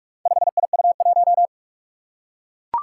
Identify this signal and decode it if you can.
Morse code, heard as a keyed tone, about 44 wpm, 700 Hz. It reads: 5IU1
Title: Four Challenging Letter-Number Combinations 45wpm